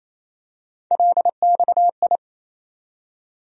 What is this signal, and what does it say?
Morse code, heard as a keyed tone, about 28 wpm, 700 Hz. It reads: LXI
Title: USA Call Sign Postfixes - 3 Letters 28wpm